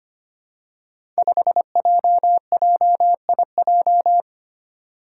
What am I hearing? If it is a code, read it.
Morse code, heard as a keyed tone, about 25 wpm, 700 Hz. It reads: 5JJIJ
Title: Five Challenging Letter-Number Combinations 25wpm